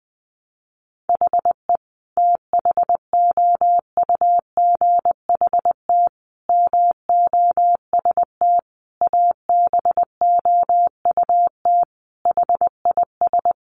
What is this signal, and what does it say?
Morse code, heard as a keyed tone, about 20 wpm, 700 Hz. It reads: HE THOUGHT MOST ABOUT HIS
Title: Sentences from Top 1500 Words 20wpm